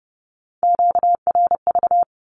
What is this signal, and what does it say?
Morse code, heard as a keyed tone, about 32 wpm, 700 Hz. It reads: QRV
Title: Q-Codes with Spoken Meaning - 30wpm